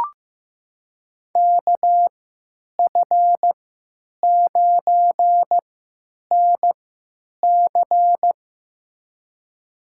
Morse code, heard as a keyed tone, about 15 wpm, spaced as for 9 wpm, 700 Hz. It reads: KF9NC
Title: USA Call Signs - 5 Characters - 3x Character Spacing - 15wpm